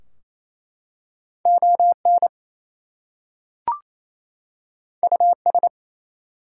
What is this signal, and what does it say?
Morse code, heard as a keyed tone, about 28 wpm, 700 Hz. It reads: ON US
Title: Two Letter Words - 28wpm